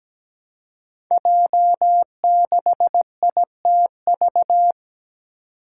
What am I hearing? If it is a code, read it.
Morse code, heard as a keyed tone, about 17 wpm, 700 Hz. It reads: J6ITV
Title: Five Challenging Letter-Number Combinations 17wpm